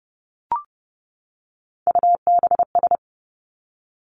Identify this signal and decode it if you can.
Morse code, heard as a keyed tone, about 30 wpm, 700 Hz. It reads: UBS
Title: Three Challenging Letter-Number Combinations 30wpm